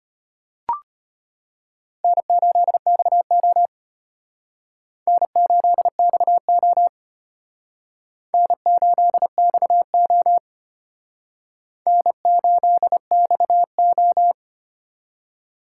Morse code, heard as a keyed tone, about 32 wpm, 700 Hz. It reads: N8XO N8XO N8XO N8XO